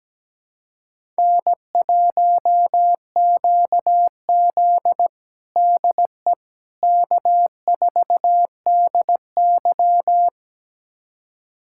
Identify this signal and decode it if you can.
Morse code, heard as a keyed tone, about 17 wpm, 700 Hz. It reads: N1QZ DE K4DY
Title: USA Callsign DE Callsign - 4 Characters - 17wpm